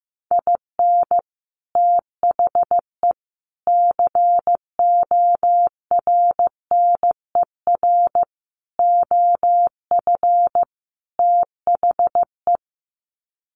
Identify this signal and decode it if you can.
Morse code, heard as a keyed tone, about 15 wpm, 700 Hz. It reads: IN THE CORNER OF THE
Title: Sets of 5 Words Encore 15wpm